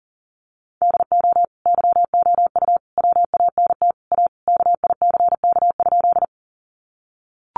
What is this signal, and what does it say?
Morse code, heard as a keyed tone, about 40 wpm, 700 Hz. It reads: DO YOU WANT A KICK?
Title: Sentences from Top 1300 Words 40wpm